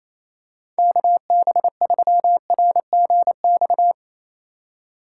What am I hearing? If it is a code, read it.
Morse code, heard as a keyed tone, about 28 wpm, 700 Hz. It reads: KB3RGX
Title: USA Call Signs - 6 Characters - 4x Character Spacing - 28wpm